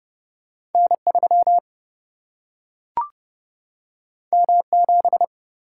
Morse code, heard as a keyed tone, about 30 wpm, 700 Hz. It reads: N 3 M 7
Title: Single-Letter Single-Number 30wpm